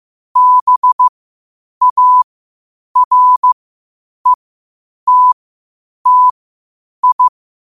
Morse code, heard as a keyed tone, about 15 wpm, spaced as for 8 wpm, 1000 Hz. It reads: BARETTI
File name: qotdfiles / 8wpm